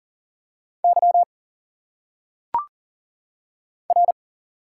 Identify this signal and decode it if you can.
Morse code, heard as a keyed tone, about 40 wpm, 700 Hz. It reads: Y R
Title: Single Letters - Phonetic Pronunciation 40wpm